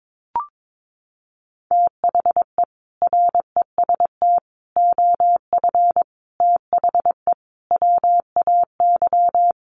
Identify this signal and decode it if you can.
Morse code, heard as a keyed tone, about 22 wpm, 700 Hz. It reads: THE REST OF THE WAY
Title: Sets of 5 Words Encore 22wpm